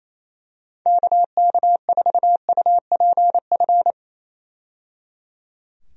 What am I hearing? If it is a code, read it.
Morse code, heard as a keyed tone, about 27 wpm, 700 Hz. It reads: KK4UPF
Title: USA Call Signs - 6 Characters - 28wpm